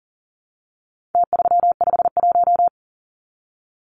Morse code, heard as a keyed tone, about 40 wpm, 700 Hz. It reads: T351
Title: Numbers - Contest Style - 4-Digit 40wpm